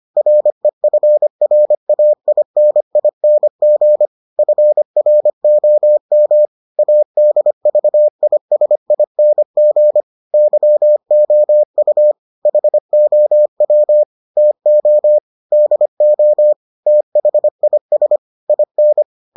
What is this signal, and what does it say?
Morse code, heard as a keyed tone, about 25 wpm, 600 Hz. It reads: REFRAINING FROM ADVISING YOU HOW TO DO THIS IN